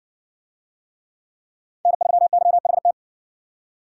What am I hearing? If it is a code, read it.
Morse code, heard as a keyed tone, about 65 wpm, 700 Hz. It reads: N3YHT